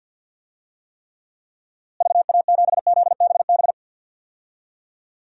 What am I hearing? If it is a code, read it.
Morse code, heard as a keyed tone, about 52 wpm, 700 Hz. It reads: VA7ZBB
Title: DXCC-100 Call Signs - 6 Characters - Repeated 2x - 50wpm